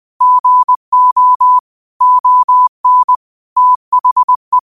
Morse code, heard as a keyed tone, about 20 wpm, 1000 Hz. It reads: GO ON THE
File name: qotdfiles / 20wpm